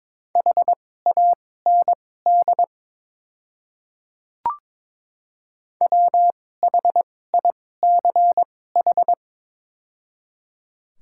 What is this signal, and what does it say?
Morse code, heard as a keyed tone, about 22 wpm, spaced as for 16 wpm, 700 Hz. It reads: HAND WHICH
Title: 2x Character Spacing - Top 200 Words - 22wpm